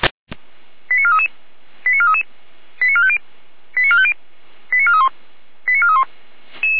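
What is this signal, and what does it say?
Five-tone selective call (ZVEI), heard as quick runs of five tones, tones of about 70 ms. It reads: 89433 89433 89450 89450 89431 89431
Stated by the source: Five-tone ZVEI selective call received on SDR.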